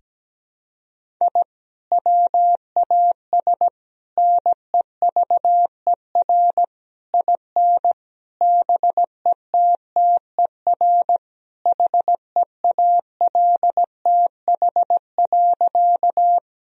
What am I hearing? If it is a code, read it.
Morse code, heard as a keyed tone, about 17 wpm, 700 Hz. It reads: I WAS NEVER IN BETTER HEALTH.
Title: Sentences from Top 1000 Words - Review Part 3 - 17wpm